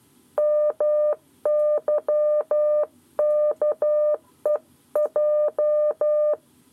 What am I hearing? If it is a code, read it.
Morse code, heard as a keyed tone, about 12 wpm, 600 Hz. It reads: MYKEJ
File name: 7 Lezione cw 1